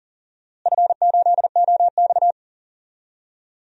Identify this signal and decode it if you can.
Morse code, heard as a keyed tone, about 40 wpm, 700 Hz. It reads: F8OX
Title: DXCC-100 Call Signs - 4 Characters - Repeated 2x - 40wpm